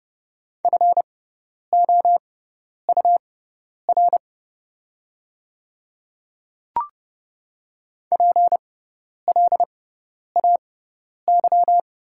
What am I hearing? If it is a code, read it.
Morse code, heard as a keyed tone, about 30 wpm, spaced as for 10 wpm, 700 Hz. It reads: FOUR PLAY
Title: 6x Character Spacing - Top 300 Words - 30wpm